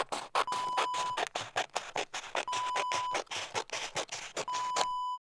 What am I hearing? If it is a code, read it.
There is no code.